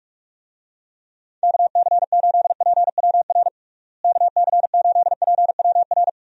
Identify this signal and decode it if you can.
Morse code, heard as a keyed tone, about 46 wpm, 700 Hz. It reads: KC8PWR KC8PWR
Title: USA Call Signs - 6 Characters - Repeated 3x - 45wpm